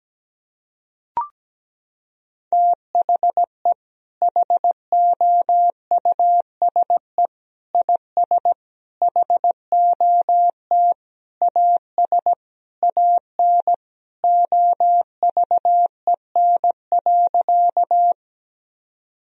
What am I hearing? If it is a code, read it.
Morse code, heard as a keyed tone, about 17 wpm, 700 Hz. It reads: THE HOUSE IS HOT AS AN OVEN.
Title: Sentences from Top 1000 Words - Review Part 4 - 17wpm